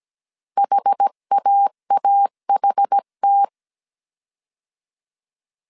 Morse code, heard as a keyed tone, about 17 wpm, 800 Hz. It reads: HAAHT